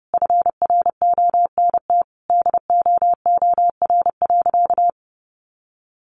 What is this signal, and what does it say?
Morse code, heard as a keyed tone, about 30 wpm, 700 Hz. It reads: FRONT DOOR.
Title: Sentences from Top 1300 Words 30wpm